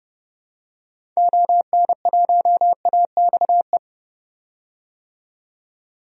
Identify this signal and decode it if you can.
Morse code, heard as a keyed tone, about 30 wpm, 700 Hz. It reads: ON1AXE
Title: DXCC-100 Call Signs - 6 Characters - 30wpm